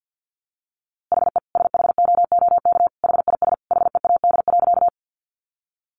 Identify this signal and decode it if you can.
Morse code, heard as a keyed tone, about 50 wpm, 700 Hz. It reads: HE SHOOK HIS HEAD.
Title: A Sound Of Thunder 50wpm